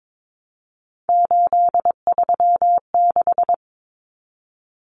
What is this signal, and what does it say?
Morse code, heard as a keyed tone, about 22 wpm, 700 Hz. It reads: 836
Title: Numbers - 3-Digit 22wpm